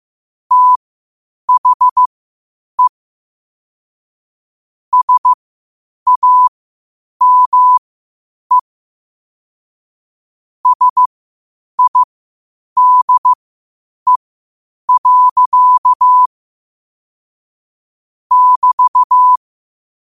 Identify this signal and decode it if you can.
Morse code, heard as a keyed tone, about 15 wpm, spaced as for 8 wpm, 1000 Hz. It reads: THE SAME SIDE. <BT>
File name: qotdfiles / 8wpm